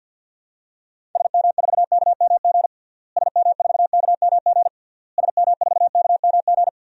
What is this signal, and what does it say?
Morse code, heard as a keyed tone, about 50 wpm, 700 Hz. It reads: SM4KMG SM4KMG SM4KMG
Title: DXCC-100 Call Signs - 6 Characters - Repeated 3x - 50wpm